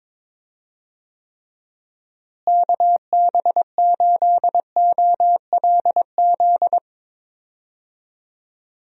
Morse code, heard as a keyed tone, about 22 wpm, 700 Hz. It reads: KB8OLZ
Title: USA Call Signs - 6 Characters - Repeated 3x - 22wpm